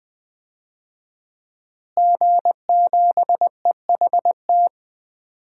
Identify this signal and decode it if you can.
Morse code, heard as a keyed tone, about 20 wpm, 700 Hz. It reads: G7EHT
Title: DXCC-100 Call Signs - 5 Characters - Repeated 2x - 20wpm